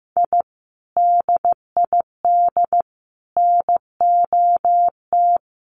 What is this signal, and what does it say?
Morse code, heard as a keyed tone, about 15 wpm, 700 Hz. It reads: I DID NOT
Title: Sets of 5 Words 15wpm